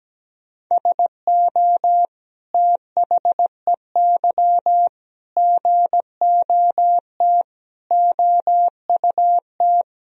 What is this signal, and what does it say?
Morse code, heard as a keyed tone, about 17 wpm, 700 Hz. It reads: SO THEY GOT OUT
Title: Sentences from Top 300 Words and Abbreviations 17wpm